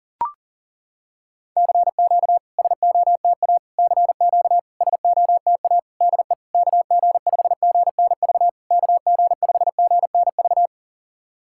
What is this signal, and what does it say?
Morse code, heard as a keyed tone, about 42 wpm, 700 Hz. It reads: CQ SOTA CQ SOTA DE KG5GNV KG5GNV
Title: SOTA QSO Practice 40wpm